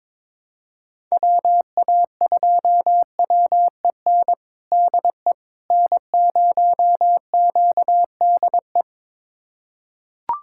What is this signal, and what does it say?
Morse code, heard as a keyed tone, about 22 wpm, 700 Hz. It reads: WA2WEN DE N0QDE
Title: USA Callsign DE Callsign 22wpm